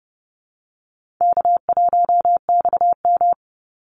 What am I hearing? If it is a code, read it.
Morse code, heard as a keyed tone, about 30 wpm, 700 Hz. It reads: K1XM
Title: Easy Call Signs - 30wpm